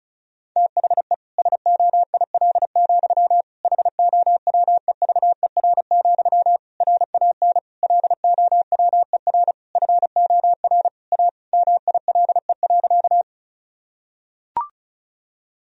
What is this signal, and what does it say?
Morse code, heard as a keyed tone, about 36 wpm, 700 Hz. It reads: THE SOIL, HOWEVER, RAN LOWER FOR A MILE.
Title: Sentences from Top 1000 Words - Review Part 4 - 35wpm